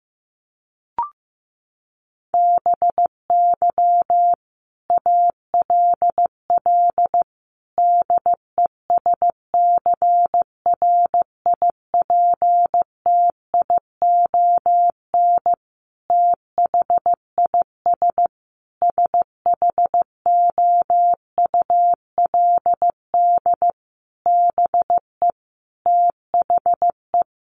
Morse code, heard as a keyed tone, about 15 wpm, 700 Hz. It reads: BY ALL DESCRIPTION THIS SHOULD BE THE
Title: Sentences from Top 1300 Words 15wpm